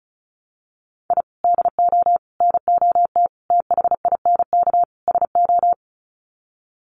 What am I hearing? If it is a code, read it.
Morse code, heard as a keyed tone, about 34 wpm, 700 Hz. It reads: I DO NOT THINK SO
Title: Sets of 5 Words 35wpm